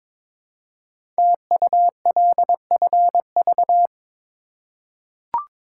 Morse code, heard as a keyed tone, about 22 wpm, 700 Hz. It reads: TULFV
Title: Code Groups - 5 Letters - 22wpm